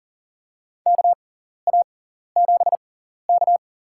Morse code, heard as a keyed tone, about 40 wpm, spaced as for 14 wpm, 700 Hz. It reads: KA7K
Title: USA Call Signs - 6x Character Spacing - 40wpm